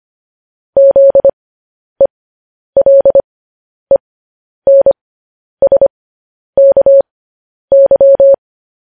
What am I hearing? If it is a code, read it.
Morse code, heard as a keyed tone, about 25 wpm, spaced as for 10 wpm, 550 Hz. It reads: ZELENSKY